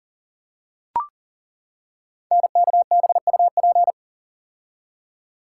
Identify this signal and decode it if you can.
Morse code, heard as a keyed tone, about 40 wpm, 700 Hz. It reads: NKBUP
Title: Code Groups - 5 Letters - 40wpm